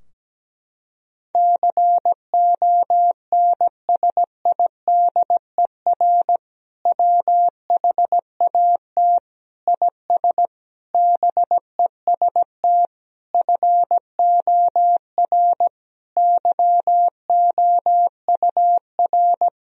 Morse code, heard as a keyed tone, about 17 wpm, 700 Hz. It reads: CONSIDER WHAT IS BEST FOR YOUR
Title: Sentences from Top 1000 Words - Review Part 4 - 17wpm